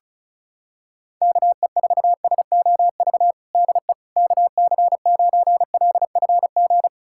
Morse code, heard as a keyed tone, about 34 wpm, 700 Hz. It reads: KE4SOV DE KC9LFG
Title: USA Callsign DE Callsign - 6 Characters - 35wpm